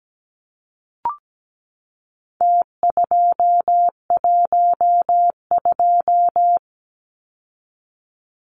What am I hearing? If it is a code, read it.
Morse code, heard as a keyed tone, about 17 wpm, 700 Hz. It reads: T212
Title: Numbers - Contest Style - 4-Digit 17wpm